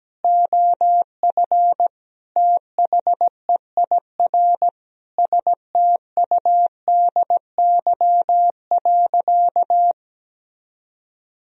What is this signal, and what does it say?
Morse code, heard as a keyed tone, about 17 wpm, 700 Hz. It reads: OF THEIR STUDY.
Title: Sentences from Top 1000 Words - Review Part 3 - 17wpm